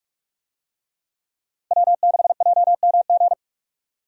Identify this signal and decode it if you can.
Morse code, heard as a keyed tone, about 47 wpm, 700 Hz. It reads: W6JMG
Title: USA Call Signs - 5 Characters - Repeated 2x - 45wpm